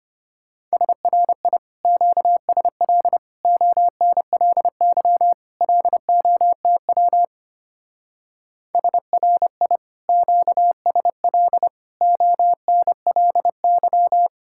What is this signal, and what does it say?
Morse code, heard as a keyed tone, about 33 wpm, 700 Hz. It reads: SRI QSL ONLY LOTW SRI QSL ONLY